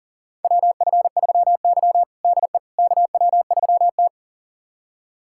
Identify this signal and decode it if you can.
Morse code, heard as a keyed tone, about 37 wpm, 700 Hz. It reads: WF3Y DE KW3T
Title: USA Callsign DE Callsign - 4 Characters - 40wpm